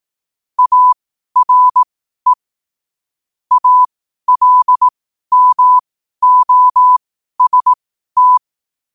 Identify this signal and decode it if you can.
Morse code, heard as a keyed tone, about 18 wpm, spaced as for 12 wpm, 1000 Hz. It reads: ARE ALMOST
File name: qotdfiles / 12wpm